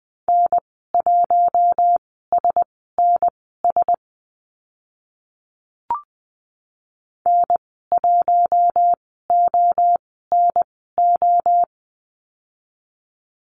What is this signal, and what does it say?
Morse code, heard as a keyed tone, about 20 wpm, spaced as for 14 wpm, 700 Hz. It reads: N1SNS N1ONO
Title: 042 Lesson 8g - Number 1 - Call Signs 20wpm